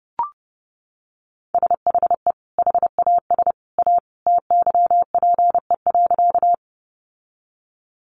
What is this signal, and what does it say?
Morse code, heard as a keyed tone, about 30 wpm, 700 Hz. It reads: SHE HAS A TYPE.
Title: Sentences from Top 600 Words 30wpm